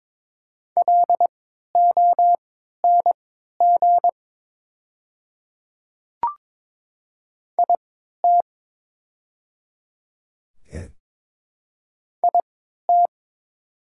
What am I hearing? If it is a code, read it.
Morse code, heard as a keyed tone, about 22 wpm, spaced as for 12 wpm, 700 Hz. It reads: LONG IT IT
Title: 3x Character Spacing - Top 100 Words - 22wpm